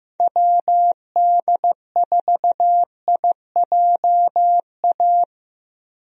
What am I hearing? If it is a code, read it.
Morse code, heard as a keyed tone, about 15 wpm, 700 Hz. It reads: WD4IJA
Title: USA Call Signs - 6 Characters - 15wpm